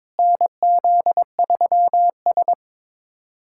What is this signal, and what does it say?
Morse code, heard as a keyed tone, about 22 wpm, 700 Hz. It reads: NZ3S
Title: USA Callsign DE Callsign - 4 Characters - 22wpm